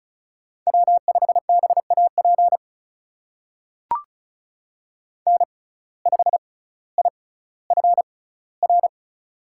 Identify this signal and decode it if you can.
Morse code, heard as a keyed tone, about 36 wpm, spaced as for 18 wpm, 700 Hz. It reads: W5BAP N5IFR
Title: USA Call Signs - 5 Characters - 6x Character Spacing - 35wpm